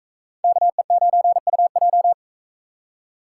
Morse code, heard as a keyed tone, about 40 wpm, 700 Hz. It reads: KE9UJ